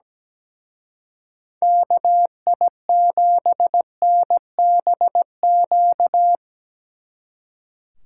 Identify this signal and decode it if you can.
Morse code, heard as a keyed tone, about 19 wpm, 700 Hz. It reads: KI7NBQ
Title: Speed Racing - USA 2x3 Call Signs 17wpm